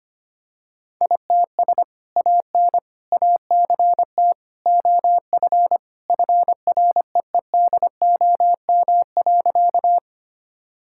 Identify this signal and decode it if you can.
Morse code, heard as a keyed tone, about 25 wpm, 700 Hz. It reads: ITS AN ACT OF FREEDOM.
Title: Sentences from Top 1000 Words - Review Part 5 - 25wpm